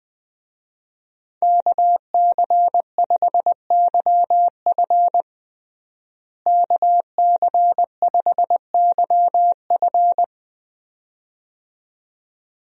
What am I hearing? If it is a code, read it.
Morse code, heard as a keyed tone, about 21 wpm, 700 Hz. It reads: KC5YF KC5YF
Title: USA Call Signs - 5 Characters - Repeated 2x - 20wpm